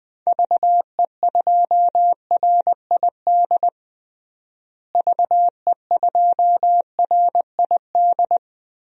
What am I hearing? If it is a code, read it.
Morse code, heard as a keyed tone, about 20 wpm, 700 Hz. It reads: VE2RID VE2RID
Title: DXCC-100 Call Signs - 6 Characters - Repeated 3x - 20wpm